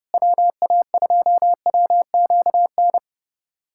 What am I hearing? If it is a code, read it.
Morse code, heard as a keyed tone, about 30 wpm, 700 Hz. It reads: WA2WQN
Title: USA Callsign DE Callsign 30wpm